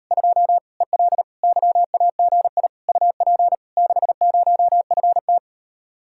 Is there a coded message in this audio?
Yes: Morse code from 0.1 to 5.4 s, heard as a keyed tone, about 39 wpm, 700 Hz. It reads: 2 EL YAGI UP 60FT